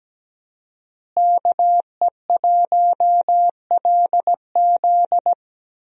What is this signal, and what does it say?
Morse code, heard as a keyed tone, about 17 wpm, 700 Hz. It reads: KE1LZ
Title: USA Call Signs - 5 Characters - Repeated 2x - 17wpm